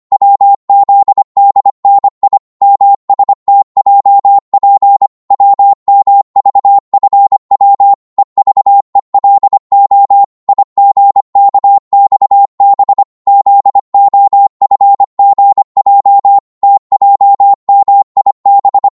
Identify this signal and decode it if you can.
Morse code, heard as a keyed tone, about 25 wpm, 800 Hz. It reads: WZDNI MSTJP WMVFW EVELO IGKXB ZOFGJ TJMIB
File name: Morse / ltr-25wpm